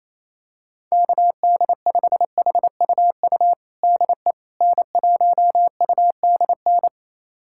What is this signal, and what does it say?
Morse code, heard as a keyed tone, about 28 wpm, 700 Hz. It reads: KD5HUU DE N1UDN
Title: USA Callsign DE Callsign 28wpm